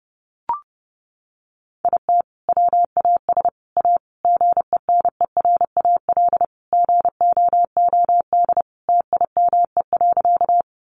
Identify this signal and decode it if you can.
Morse code, heard as a keyed tone, about 30 wpm, 700 Hz. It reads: IT WAS A GENERAL GOOD TIME.
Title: Sentences from Top 600 Words 30wpm